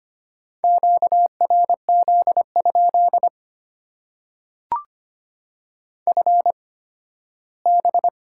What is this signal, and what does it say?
Morse code, heard as a keyed tone, about 25 wpm, spaced as for 15 wpm, 700 Hz. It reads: QRZ? FB
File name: Top 90 QSO Elements - 8x Character Spacing - 25wpm